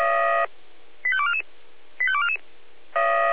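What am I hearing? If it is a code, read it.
Five-tone selective call (ZVEI), heard as quick runs of five tones, tones of about 70 ms. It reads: 85244 85244
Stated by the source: Five-tone ZVEI selective call 85214 received on an SDR.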